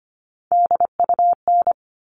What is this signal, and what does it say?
Morse code, heard as a keyed tone, about 25 wpm, 700 Hz. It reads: DUN
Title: All Cut Numbers - 3-Digit 25wpm